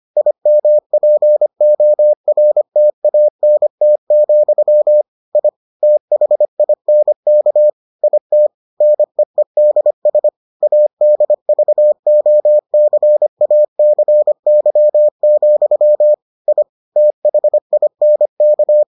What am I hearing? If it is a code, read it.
Morse code, heard as a keyed tone, about 25 wpm, 600 Hz. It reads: IMPORTANT, I THINK IT NEEDS ADVOCACY, I THINK